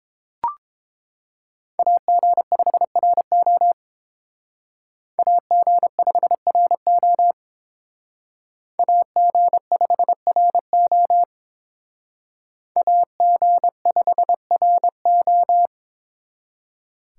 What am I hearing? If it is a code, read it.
Morse code, heard as a keyed tone, about 28 wpm, 700 Hz. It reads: AG5RO AG5RO AG5RO AG5RO